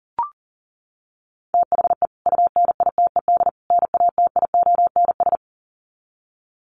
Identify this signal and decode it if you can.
Morse code, heard as a keyed tone, about 40 wpm, 700 Hz. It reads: THE UNITED NATIONS
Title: Sets of 3 Words 40wpm